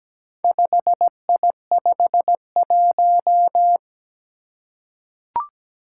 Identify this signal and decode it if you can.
Morse code, heard as a keyed tone, about 17 wpm, 700 Hz. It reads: 5I51
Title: Four Challenging Letter-Number Combinations 17wpm